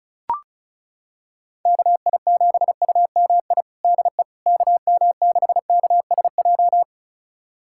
Morse code, heard as a keyed tone, about 34 wpm, 700 Hz. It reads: KI7UMI DE KM6KSJ
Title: USA Callsign DE Callsign - 6 Characters - 35wpm